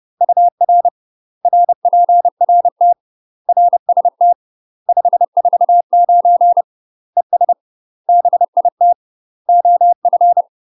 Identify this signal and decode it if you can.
Morse code, heard as a keyed tone, about 30 wpm, 700 Hz. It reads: UR RPRT RST 549 ES BIT OF
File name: UK MASTER QSOs TRAINING / 30WPM / UK MASTER QSOs 0000